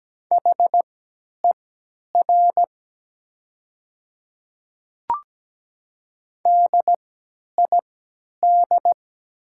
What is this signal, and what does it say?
Morse code, heard as a keyed tone, about 17 wpm, spaced as for 10 wpm, 700 Hz. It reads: HER DID
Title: 3x Character Spacing - Top 100 Words - 17wpm